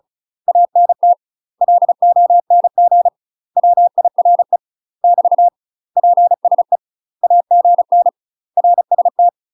Morse code, heard as a keyed tone, about 34 wpm, 700 Hz. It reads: ANT LONG WIRE <BT> PSE AGN RST
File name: UK MASTER QSOs TRAINING / 35WPM / UK MASTER QSOs 0030